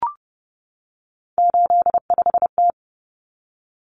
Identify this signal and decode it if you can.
Morse code, heard as a keyed tone, about 28 wpm, 700 Hz. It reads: 85T
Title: Common Cut Numbers - 3-Digit 30wpm